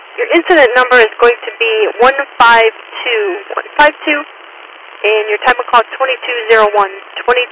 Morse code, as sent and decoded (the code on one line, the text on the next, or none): none